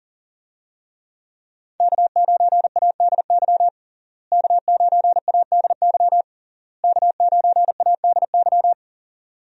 Morse code, heard as a keyed tone, about 40 wpm, 700 Hz. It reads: K9ADY K9ADY K9ADY
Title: USA Call Signs - 5 Characters - Repeated 3x - 40wpm